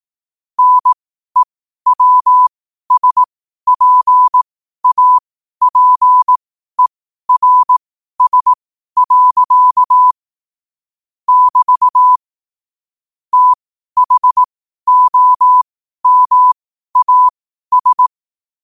Morse code, heard as a keyed tone, about 18 wpm, spaced as for 12 wpm, 1000 Hz. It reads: NEWSPAPERS. <BT> THOMAS
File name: qotdfiles / 12wpm